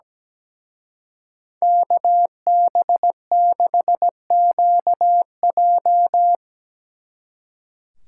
Morse code, heard as a keyed tone, about 19 wpm, 700 Hz. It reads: KB6QJ
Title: Speed Racing - USA 2x2 Call Signs 17wpm